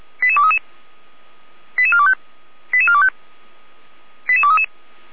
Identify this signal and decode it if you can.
Five-tone selective call (ZVEI), heard as quick runs of five tones, tones of about 70 ms. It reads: 88233 88425 88425 88233